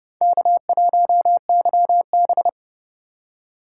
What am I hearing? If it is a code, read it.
Morse code, heard as a keyed tone, about 29 wpm, 700 Hz. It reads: K1YB